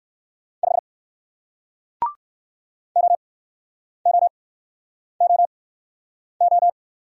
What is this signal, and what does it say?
Morse code, heard as a keyed tone, about 55 wpm, spaced as for 9 wpm, 700 Hz. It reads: 5 OOOO
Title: Speed-Racing Single Letter-Number 45wpm